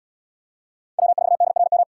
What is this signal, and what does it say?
Morse code, heard as a keyed tone, about 69 wpm, 700 Hz. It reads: W4DRM